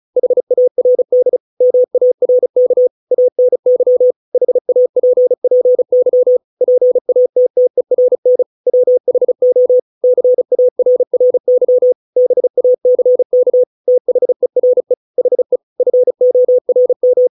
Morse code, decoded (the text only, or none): HARD MARK ANY HAPPY PATTERN WHO CARRY BACK THERE HE FORM